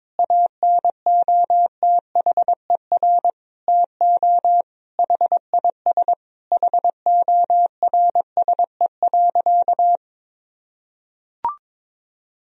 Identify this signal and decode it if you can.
Morse code, heard as a keyed tone, about 22 wpm, 700 Hz. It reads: ANOTHER TO HIS HORSE.
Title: Sentences from Top 1000 Words - Review Part 1 - 22wpm